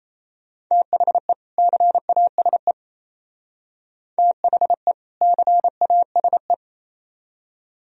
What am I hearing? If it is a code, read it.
Morse code, heard as a keyed tone, about 32 wpm, 700 Hz. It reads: THE CASE THE CASE
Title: Speed-Racing - Sets of 2 Words 28wpm